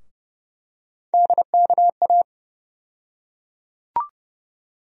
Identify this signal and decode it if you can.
Morse code, heard as a keyed tone, about 31 wpm, 700 Hz. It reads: DKA
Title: USA Call Sign Postfixes - 3 Letters 30wpm